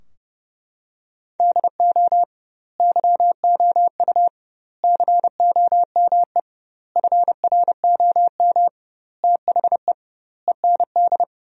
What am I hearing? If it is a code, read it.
Morse code, heard as a keyed tone, about 30 wpm, 700 Hz. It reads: DO YOU COME FROM THE END
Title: 2x Word-Spacing - Sentences from Top 200 Words - 30wpm